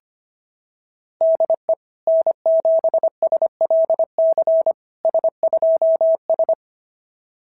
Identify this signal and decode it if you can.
Morse code, heard as a keyed tone, about 25 wpm, 660 Hz. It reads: DE N7SLC S2S
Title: SOTA QSO Practice 25wpm